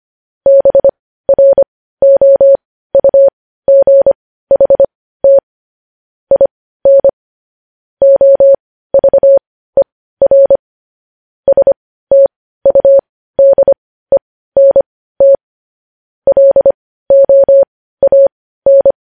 Morse code, heard as a keyed tone, about 25 wpm, spaced as for 15 wpm, 550 Hz. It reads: BROUGHT IN OVER STUDENT LOAN
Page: news-headlines-in-morse-code-at-15-wpm